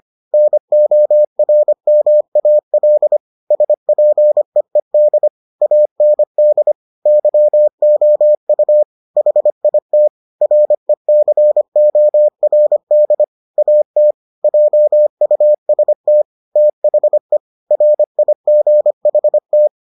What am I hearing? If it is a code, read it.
Morse code, heard as a keyed tone, about 25 wpm, 600 Hz. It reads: NORMAL SPEED AND YOU HIT RECORD AT JUST THE RIGHT